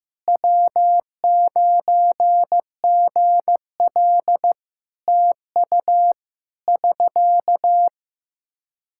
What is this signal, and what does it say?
Morse code, heard as a keyed tone, about 15 wpm, 700 Hz. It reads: W9GL TU <SK>
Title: K1USN SST QSO Practice 15wpm